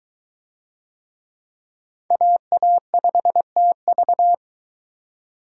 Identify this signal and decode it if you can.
Morse code, heard as a keyed tone, about 21 wpm, 700 Hz. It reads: AA5TV